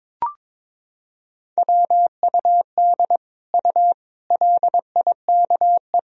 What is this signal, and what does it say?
Morse code, heard as a keyed tone, about 22 wpm, 700 Hz. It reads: WUD U LIKE
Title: Sentences from Top 100 Words and Abbreviations 22wpm